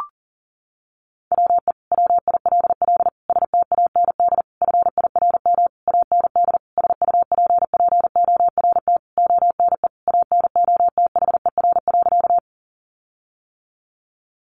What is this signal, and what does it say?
Morse code, heard as a keyed tone, about 37 wpm, 700 Hz. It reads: WE WILL STAND FIRM AND SUPPORT ONE ANOTHER.
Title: Sentences from Top 1400 Words 40wpm